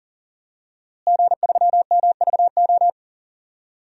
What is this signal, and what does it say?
Morse code, heard as a keyed tone, about 40 wpm, 700 Hz. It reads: G3MVO
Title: DXCC-100 Call Signs - 5 Characters - Repeated 2x - 40wpm